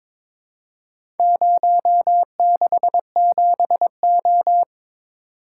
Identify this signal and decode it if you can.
Morse code, heard as a keyed tone, about 22 wpm, 700 Hz. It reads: 067O
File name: Four Letter-Number Combinations - 22wpm